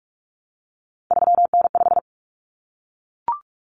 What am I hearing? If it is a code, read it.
Morse code, heard as a keyed tone, about 44 wpm, 700 Hz. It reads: 3N5
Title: Common Cut Numbers - 3-Digit 45wpm